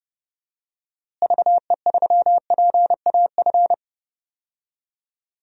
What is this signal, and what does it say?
Morse code, heard as a keyed tone, about 30 wpm, 700 Hz. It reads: VE3PAF
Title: DXCC-100 Call Signs - 6 Characters - Repeated 3x - 30wpm